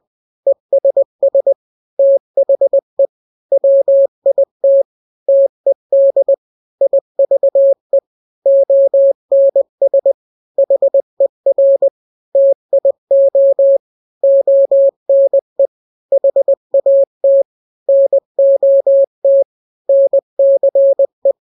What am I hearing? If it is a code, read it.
Morse code, heard as a keyed tone, about 20 wpm, 550 Hz. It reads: ESS THE WIT TED IVE ONS HER TIO ONE HAT NOT NCE
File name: n-grams / 3-grams 20 18 WPM